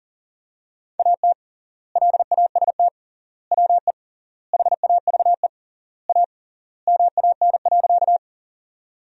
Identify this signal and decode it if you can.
Morse code, heard as a keyed tone, about 40 wpm, 700 Hz. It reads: AT LAST WE HAVE A MAN.
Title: 3x Word-Spacing - Sentences from Top 300 Words - 40wpm